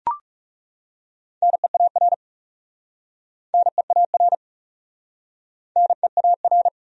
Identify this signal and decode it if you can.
Morse code, heard as a keyed tone, about 37 wpm, 700 Hz. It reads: NEAR NEAR NEAR
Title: Speed-Racing Top 200 Words 30wpm